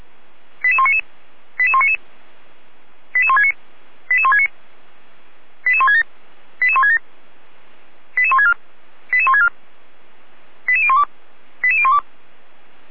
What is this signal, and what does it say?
Five-tone selective call (ZVEI), heard as quick runs of five tones, tones of about 70 ms. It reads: 88199 88199 88169 88169 88167 88167 88164 88164 88012 88012
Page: SDR/ZVEI/88122